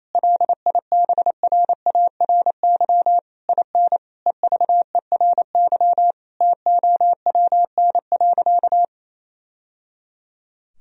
Morse code, heard as a keyed tone, about 28 wpm, 700 Hz. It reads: LIBRARY IN EVERY TOWN.
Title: Sentences from Top 1000 Words - Review Part 4 - 28wpm